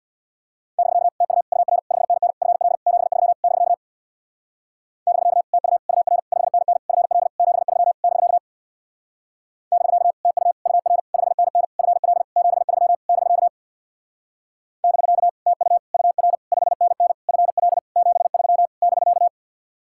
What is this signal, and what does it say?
Morse code, heard as a keyed tone, about 64 wpm, 700 Hz. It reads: <BK> TU UR 5NN FL 73 <BK> <BK> TU UR 5NN FL 73 <BK> <BK> TU UR 5NN FL 73 <BK> <BK> TU UR 5NN FL 73 <BK>